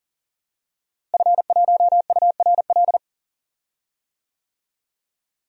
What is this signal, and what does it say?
Morse code, heard as a keyed tone, about 40 wpm, 700 Hz. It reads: F1URL
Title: DXCC-100 Call Signs - 5 Characters - Repeated 3x - 40wpm